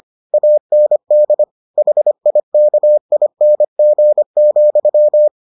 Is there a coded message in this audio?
Yes: Morse code from 0.3 to 5.3 s, heard as a keyed tone, about 25 wpm, 600 Hz. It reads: AND HIKING,